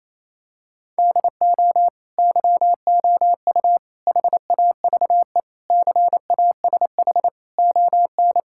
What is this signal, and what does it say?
Morse code, heard as a keyed tone, about 28 wpm, 700 Hz. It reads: DO YOU HAVE CASH ON
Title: Sentences from Top 1000 Words - Review Part 5 - 28wpm